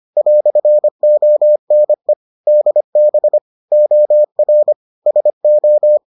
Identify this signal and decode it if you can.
Morse code, heard as a keyed tone, about 25 wpm, 600 Hz. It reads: "ONE DB OR SO